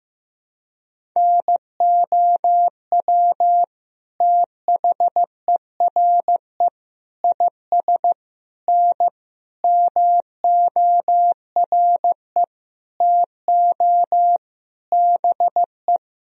Morse code, heard as a keyed tone, about 15 wpm, 700 Hz. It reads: NOW THERE IS N MORE TO BE
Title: Sentences from Top 100 Words and Abbreviations 15wpm